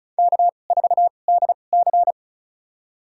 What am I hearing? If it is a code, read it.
Morse code, heard as a keyed tone, about 36 wpm, spaced as for 25 wpm, 700 Hz. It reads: K4DC
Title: USA Call Signs - 2x Character Spacing - 35wpm